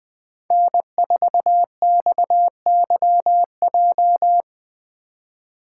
Morse code, heard as a keyed tone, about 20 wpm, 700 Hz. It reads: N4XYJ
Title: USA Call Signs - 5 Characters - 3x Character Spacing - 20wpm